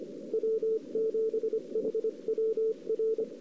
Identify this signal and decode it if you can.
Morse code, heard as a keyed tone, about 22 wpm, 410 Hz. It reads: W7HWR
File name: CW / 2008-ss-cw-k5tr-op-n5rz -04:00